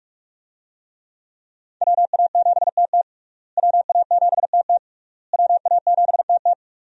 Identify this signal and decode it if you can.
Morse code, heard as a keyed tone, about 46 wpm, 700 Hz. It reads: WA7TT WA7TT WA7TT
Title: USA Call Signs - 5 Characters - Repeated 3x - 45wpm